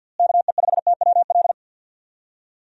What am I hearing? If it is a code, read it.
Morse code, heard as a keyed tone, about 50 wpm, 700 Hz. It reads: KE5TWL
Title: USA Call Signs - 6 Characters - Repeated 3x - 50wpm